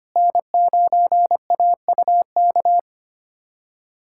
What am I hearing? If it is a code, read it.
Morse code, heard as a keyed tone, about 25 wpm, 700 Hz. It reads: N9AUK
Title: USA Callsign DE Callsign - 5 Characters - 25wpm